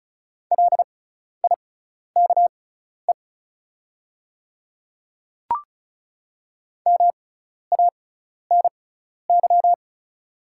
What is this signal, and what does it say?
Morse code, heard as a keyed tone, about 36 wpm, spaced as for 12 wpm, 690 Hz. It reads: LIKE MANY
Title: 6x Character Spacing - Top 100 Words - 35wpm